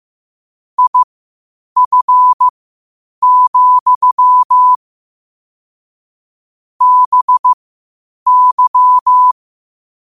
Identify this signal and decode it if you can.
Morse code, heard as a keyed tone, about 15 wpm, spaced as for 8 wpm, 1000 Hz. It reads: IF, BY
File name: qotdfiles / 8wpm